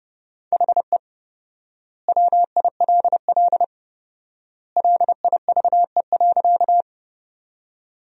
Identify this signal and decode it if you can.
Morse code, heard as a keyed tone, about 30 wpm, 700 Hz. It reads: HE WILL LIVE.
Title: 4x Word-Spacing - Sentences from Top 200 Words - 30wpm